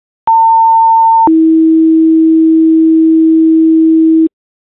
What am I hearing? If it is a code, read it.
Two-tone page, heard as a one-second tone, then a three-second tone, about 900 then 320 Hz, with noise. It is heard from 0.3 to 4.3 s.